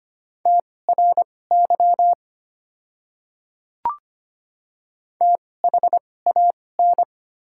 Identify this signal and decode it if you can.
Morse code, heard as a keyed tone, about 25 wpm, spaced as for 18 wpm, 690 Hz. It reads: TRY THAN
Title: 2x Character Spacing - Top 200 Words - 25wpm